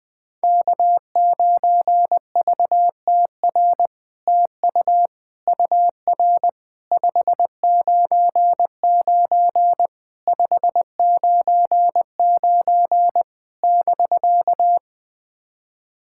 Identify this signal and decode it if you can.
Morse code, heard as a keyed tone, about 20 wpm, 700 Hz. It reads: K9VTR TU UR 599 599 <BK>
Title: SOTA QSO Practice 20wpm